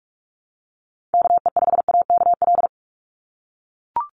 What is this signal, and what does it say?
Morse code, heard as a keyed tone, about 44 wpm, 700 Hz. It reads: KE5AKL
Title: NA SOTA Activator Callsigns 45wpm